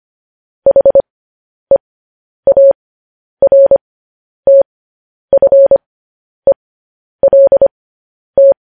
Morse code, heard as a keyed tone, about 25 wpm, spaced as for 10 wpm, 550 Hz. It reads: HEARTFELT
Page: News Headlines in Morse Code at 10 WPM